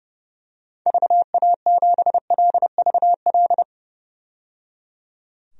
Morse code, heard as a keyed tone, about 30 wpm, 700 Hz. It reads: VA7LVL
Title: DXCC-100 Call Signs - 6 Characters - 30wpm